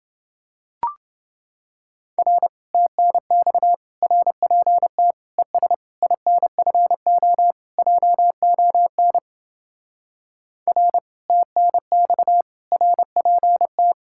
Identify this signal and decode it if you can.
Morse code, heard as a keyed tone, about 28 wpm, 700 Hz. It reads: R TNX RPT ES INFO JON R TNX RPT